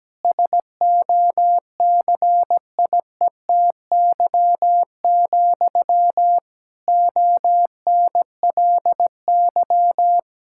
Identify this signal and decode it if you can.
Morse code, heard as a keyed tone, about 17 wpm, 700 Hz. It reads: SOCIETY, ONLY
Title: Sentences from Top 1000 Words - Review Part 6 - 17wpm